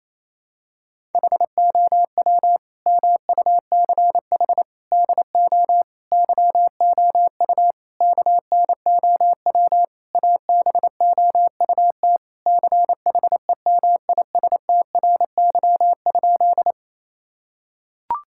Morse code, heard as a keyed tone, about 28 wpm, 700 Hz. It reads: HOW MUCH DO YOU KNOW ABOUT CHEMISTRY?
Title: Sentences from Top 1000 Words - Review Part 6 - 28wpm